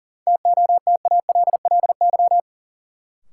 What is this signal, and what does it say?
Morse code, heard as a keyed tone, about 40 wpm, 700 Hz. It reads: TOTALLY
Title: Sentences from Top 1600 Words - New Word Drill Alt - 40wpm